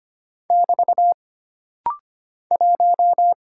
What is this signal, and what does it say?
Morse code, heard as a keyed tone, about 25 wpm, 700 Hz. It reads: <BT> 1
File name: Single Character (Letter:Number:Punctuation:Prosign) - Rapid-Fire - 25wpm